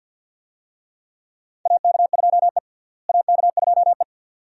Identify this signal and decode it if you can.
Morse code, heard as a keyed tone, about 46 wpm, 700 Hz. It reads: AK2E AK2E
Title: USA Call Signs - 4 Characters - Repeated 2x - 50wpm